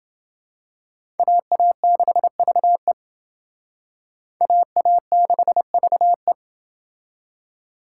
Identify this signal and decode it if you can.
Morse code, heard as a keyed tone, about 28 wpm, 700 Hz. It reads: AA6VE AA6VE